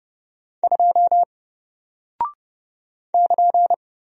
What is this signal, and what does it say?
Morse code, heard as a keyed tone, about 30 wpm, 700 Hz. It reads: 2 <KN>
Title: Single Character (Letter:Number:Punctuation:Prosign) Alt 30wpm